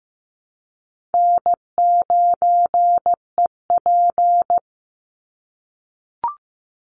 Morse code, heard as a keyed tone, about 15 wpm, 700 Hz. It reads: N9EP
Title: Easy Call Signs - 15wpm